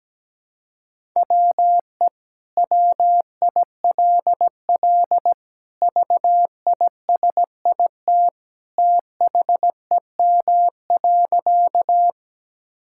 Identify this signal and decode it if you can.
Morse code, heard as a keyed tone, about 17 wpm, 700 Hz. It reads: WE WILL VISIT THEM.
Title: Sentences from Top 1000 Words - Review Part 6 - 17wpm